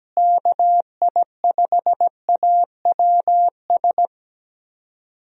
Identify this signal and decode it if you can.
Morse code, heard as a keyed tone, about 17 wpm, 700 Hz. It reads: KI5AWS
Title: USA Call Signs - 6 Characters - Repeated 2x - 17wpm